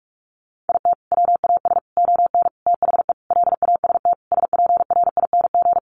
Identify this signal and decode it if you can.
Morse code, heard as a keyed tone, about 44 wpm, 700 Hz. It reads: IT WAS ON THE LAST SPRING
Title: Sentences from Top 1200 Words 45wpm